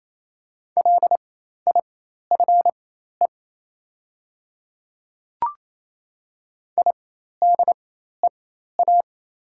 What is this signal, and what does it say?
Morse code, heard as a keyed tone, about 27 wpm, spaced as for 13 wpm, 700 Hz. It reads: LIFE IDEA
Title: 4x Character Spacing - Top 300 Words - 28wpm